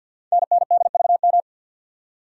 Morse code, heard as a keyed tone, about 47 wpm, 700 Hz. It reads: NNDVM
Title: Code Groups - 5 Letters - 50wpm